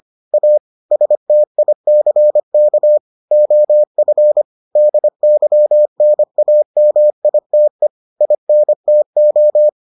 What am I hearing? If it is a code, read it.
Morse code, heard as a keyed tone, about 25 wpm, 600 Hz. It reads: A STICK OF DYNAMITE INTO